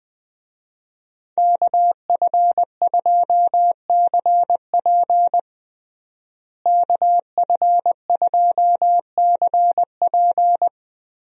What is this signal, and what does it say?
Morse code, heard as a keyed tone, about 21 wpm, 700 Hz. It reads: KF2CP KF2CP
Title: USA Call Signs - 5 Characters - Repeated 3x - 20wpm